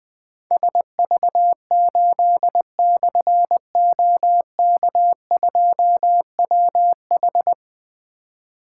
Morse code, heard as a keyed tone, about 20 wpm, 700 Hz. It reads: SV8/OK2WH
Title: Difficult Call Signs 20wpm